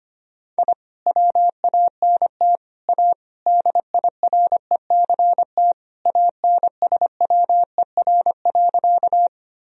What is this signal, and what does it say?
Morse code, heard as a keyed tone, about 25 wpm, 700 Hz. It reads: I WANT A DIRECT ANSWER.
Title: Sentences from Top 1000 Words - Review Part 2 - 25wpm